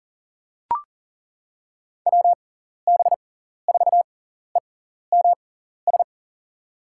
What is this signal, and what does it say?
Morse code, heard as a keyed tone, about 40 wpm, spaced as for 14 wpm, 700 Hz. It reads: WB4EMS
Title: USA Call Signs - 6x Character Spacing - 40wpm